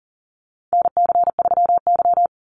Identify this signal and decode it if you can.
Morse code, heard as a keyed tone, about 40 wpm, 700 Hz. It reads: NC3Y
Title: Call Signs Repeated 2x - 40wpm